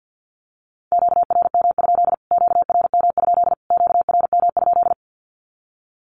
Morse code, heard as a keyed tone, about 50 wpm, 700 Hz. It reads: QRM? QRM? QRM?
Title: Q-Codes Repeated 3x 50wpm